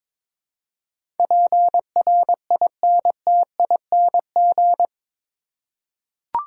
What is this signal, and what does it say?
Morse code, heard as a keyed tone, about 22 wpm, 700 Hz. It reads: PRINTING
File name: practice / 22wpm / Eight Letter Words - 22wpm